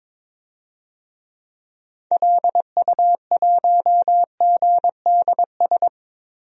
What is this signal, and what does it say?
Morse code, heard as a keyed tone, about 22 wpm, 700 Hz. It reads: LU1GDS
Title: DXCC-100 Call Signs - 6 Characters - Repeated 3x - 22wpm